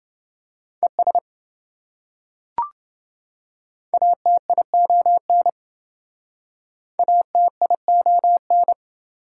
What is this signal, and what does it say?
Morse code, heard as a keyed tone, about 27 wpm, 700 Hz. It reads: ES ATION ATION